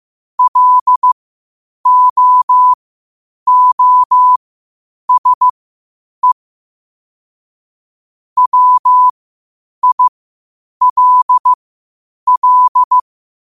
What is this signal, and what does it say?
Morse code, heard as a keyed tone, about 15 wpm, spaced as for 8 wpm, 1000 Hz. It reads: LOOSE WILL
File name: qotdfiles / 8wpm